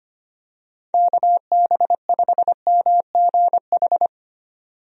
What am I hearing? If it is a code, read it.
Morse code, heard as a keyed tone, about 25 wpm, 700 Hz. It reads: KB5MGH
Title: USA Call Signs - 6 Characters - 25wpm